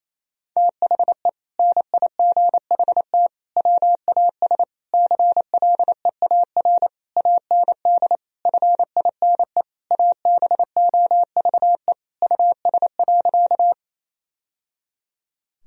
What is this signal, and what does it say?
Morse code, heard as a keyed tone, about 28 wpm, 700 Hz. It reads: THE NIGHT WAS CLEAR AND FINE ABOVE US.
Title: Sentences from Top 1000 Words - Review Part 2 - 28wpm